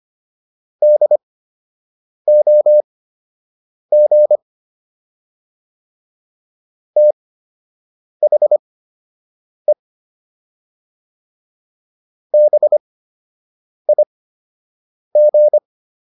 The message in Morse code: DOG THE BIG

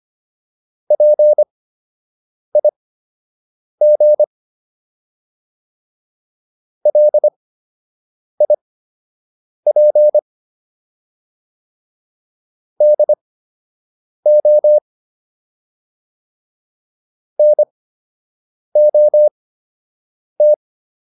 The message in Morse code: PIG LIP DO NOT

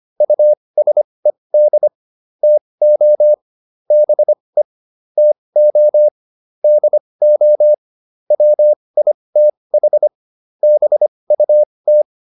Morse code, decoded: USED TO BE TO DO WITH BUT

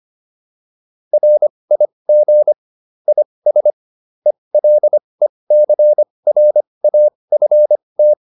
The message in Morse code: RIG IS ELECRAFT

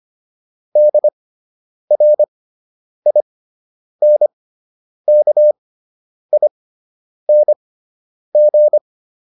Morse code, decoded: DRINKING

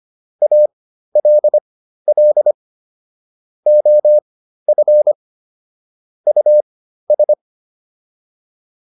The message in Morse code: ALL OF US

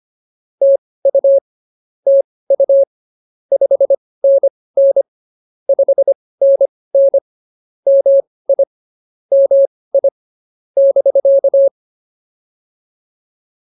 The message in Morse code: TU TU 5NN 5NN MI MI <BK>

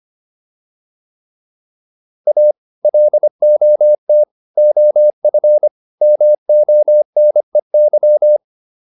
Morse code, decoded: A LOT OF MONEY

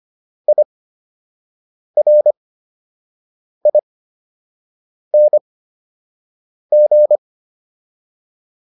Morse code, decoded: IRING

